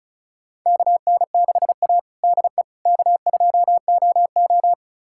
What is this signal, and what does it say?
Morse code, heard as a keyed tone, about 36 wpm, 700 Hz. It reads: KN6A DE K2OO
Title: USA Callsign DE Callsign - 4 Characters - 35wpm